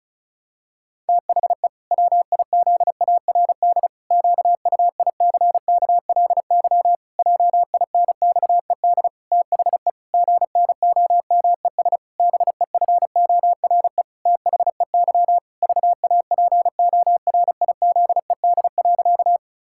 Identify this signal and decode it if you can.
Morse code, heard as a keyed tone, about 34 wpm, 700 Hz. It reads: THE WIZARD QUICKLY JINXED THE GNOMES BEFORE THEY VAPORIZED.
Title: Pangram Challenge v2 - 35wpm